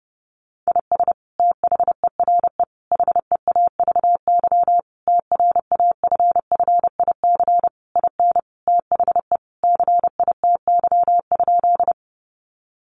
Morse code, heard as a keyed tone, about 30 wpm, 700 Hz. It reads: IS THERE HEAVY TRAFFIC IN THE CITY?
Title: Sentences from Top 1300 Words 30wpm